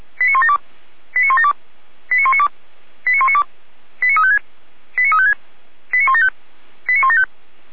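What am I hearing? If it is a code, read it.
Five-tone selective call (ZVEI), heard as quick runs of five tones, tones of about 70 ms. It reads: 89172 89172 89192 89192 89356 89356 89175 89175
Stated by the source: ZVEI five-tone selective call received by SDR, from the Feuerwehr Orlamünde audio server.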